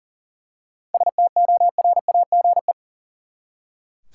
Morse code, heard as a keyed tone, about 40 wpm, 700 Hz. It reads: STORAGE
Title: Seven Letter Words - 40wpm